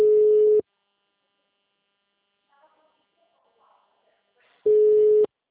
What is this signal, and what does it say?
Ringback tone (Europe, 425 Hz), with speech.
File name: out- -2013- - - .70